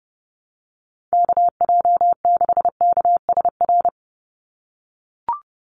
Morse code, heard as a keyed tone, about 30 wpm, 700 Hz. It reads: KJ6KSR
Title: USA Call Signs 30wpm